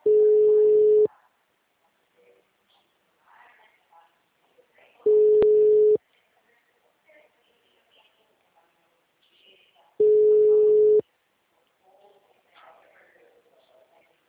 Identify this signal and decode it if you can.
Ringback tone (Europe, 425 Hz), with speech.
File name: out- -2026- - - .39974